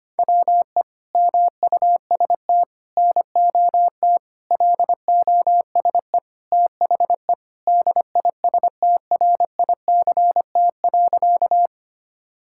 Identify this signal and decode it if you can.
Morse code, heard as a keyed tone, about 25 wpm, 700 Hz. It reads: WE MUST NOT LOSE THE DISTRICT.
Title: Sentences from Top 1600 Words 25wpm